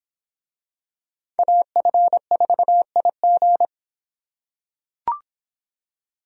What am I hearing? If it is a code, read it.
Morse code, heard as a keyed tone, about 24 wpm, 700 Hz. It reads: AF4IG